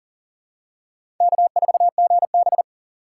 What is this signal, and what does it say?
Morse code, heard as a keyed tone, about 40 wpm, 700 Hz. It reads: K4GB
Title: USA Call Signs - 4 Characters - 3x Character Spacing - 40wpm